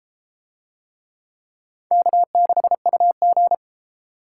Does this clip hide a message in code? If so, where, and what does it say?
Yes: Morse code from 1.9 to 3.5 s, heard as a keyed tone, about 33 wpm, 700 Hz. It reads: K6UG